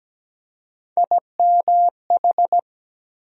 Morse code, heard as a keyed tone, about 17 wpm, 700 Hz. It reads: IMH
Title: USA Call Sign Postfixes - 3 Letters 17wpm